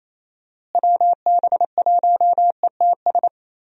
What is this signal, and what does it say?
Morse code, heard as a keyed tone, about 28 wpm, 700 Hz. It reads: WB1ETS
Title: USA Call Signs - 6 Characters - Repeated 2x - 28wpm